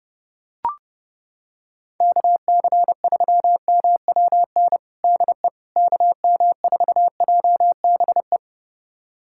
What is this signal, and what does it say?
Morse code, heard as a keyed tone, about 30 wpm, 700 Hz. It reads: KC3MWN DE KM4JBE
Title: USA Callsign DE Callsign - 6 Characters - 30wpm